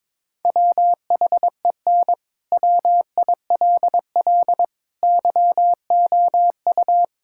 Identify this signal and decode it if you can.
Morse code, heard as a keyed tone, about 21 wpm, 700 Hz. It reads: WHEN WILL YOU
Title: Sentences from Top 1000 Words - Review Part 6 - 22wpm